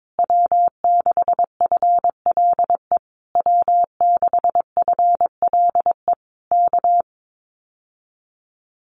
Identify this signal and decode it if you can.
Morse code, heard as a keyed tone, about 22 wpm, 700 Hz. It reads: W6FLE W6FLE K
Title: Common QSO Phrases Encore 22wpm